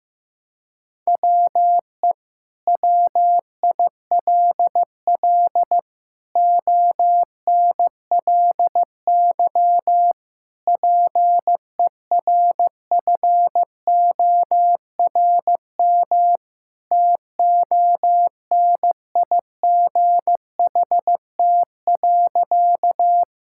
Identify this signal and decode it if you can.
Morse code, heard as a keyed tone, about 15 wpm, 700 Hz. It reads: WE WILL ONLY PERFORM TONIGHT.
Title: Sentences from Top 1000 Words - Review Part 5 - 15wpm